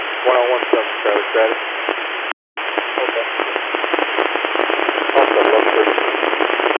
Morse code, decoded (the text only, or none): none